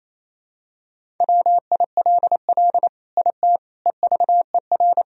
Morse code, heard as a keyed tone, about 28 wpm, 700 Hz. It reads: WILL IT EVER
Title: Sentences from Top 1000 Words - Review Part 2 - 28wpm